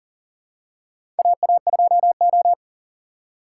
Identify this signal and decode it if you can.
Morse code, heard as a keyed tone, about 40 wpm, 700 Hz. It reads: AA2O
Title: USA Call Signs - 4 Characters - 2x Character Spacing - 40wpm